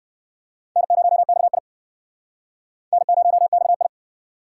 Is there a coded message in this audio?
Yes: Morse code from 0.8 to 3.9 s, heard as a keyed tone, about 57 wpm, 700 Hz. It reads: N0XI N0XI